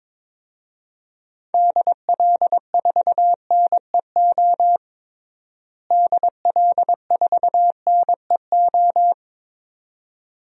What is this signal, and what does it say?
Morse code, heard as a keyed tone, about 21 wpm, 700 Hz. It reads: DL4NEO DL4NEO
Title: DXCC-100 Call Signs - 6 Characters - Repeated 2x - 22wpm